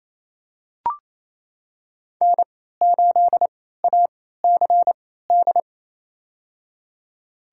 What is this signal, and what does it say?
Morse code, heard as a keyed tone, about 28 wpm, spaced as for 16 wpm, 700 Hz. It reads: N8ACD
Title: USA Call Signs - 3x Character Spacing - 28wpm